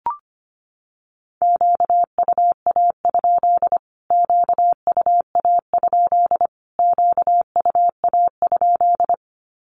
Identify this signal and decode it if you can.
Morse code, heard as a keyed tone, about 25 wpm, 700 Hz. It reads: QUA? QUA? QUA?
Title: Q-Codes Repeated 3x 25wpm